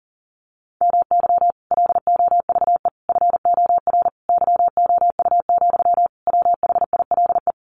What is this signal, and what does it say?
Morse code, heard as a keyed tone, about 40 wpm, 700 Hz. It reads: MY LOVE FOR YOU, WHILE
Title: Sentences from Top 1400 Words 40wpm